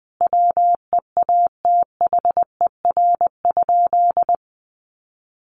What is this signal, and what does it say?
Morse code, heard as a keyed tone, about 19 wpm, 700 Hz. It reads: WEATHER?
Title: Sentences from Top 800 Words 20wpm